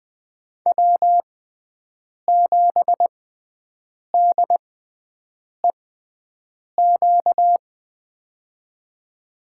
Morse code, heard as a keyed tone, about 20 wpm, spaced as for 7 wpm, 700 Hz. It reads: W7DEQ
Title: USA Call Signs - 5 Characters - 6x Character Spacing - 20wpm